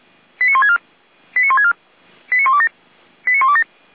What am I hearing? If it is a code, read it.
Five-tone selective call (ZVEI), heard as quick runs of five tones, tones of about 70 ms. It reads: 89164 89164 89127 89127